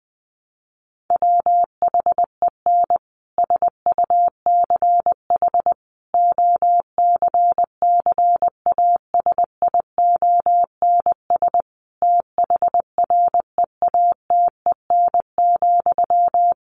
Morse code, heard as a keyed tone, about 20 wpm, 700 Hz. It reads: WHEN SUCH OCCASIONS THREATEN,
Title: A Sound Of Thunder 20wpm